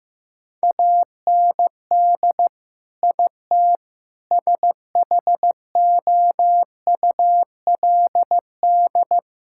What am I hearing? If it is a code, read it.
Morse code, heard as a keyed tone, about 15 wpm, 700 Hz. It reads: AND IT SHOULD
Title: Sentences from Top 1000 Words - Review Part 6 - 15wpm